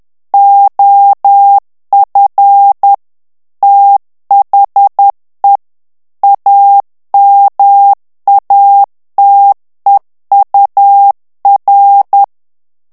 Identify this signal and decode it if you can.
Morse code, heard as a keyed tone, about 10 wpm, 800 Hz. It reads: OF THE AMATEUR
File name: ANGLAIS texte clair 3 vitesse 10 mots minute